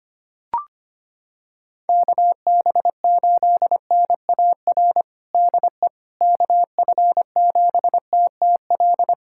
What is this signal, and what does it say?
Morse code, heard as a keyed tone, about 25 wpm, 700 Hz. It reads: KB8NAR DE KF7TTL
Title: USA Callsign DE Callsign - 6 Characters - 25wpm